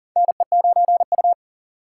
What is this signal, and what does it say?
Morse code, heard as a keyed tone, about 40 wpm, 700 Hz. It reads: NE9U
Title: K1USN SST QSO Practice - 40wpm